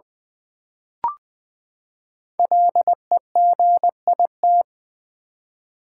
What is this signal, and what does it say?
Morse code, heard as a keyed tone, about 20 wpm, 700 Hz. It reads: LEGIT
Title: Build A Word - 5 Letters At A Time - 10-Letter Words - 20wpm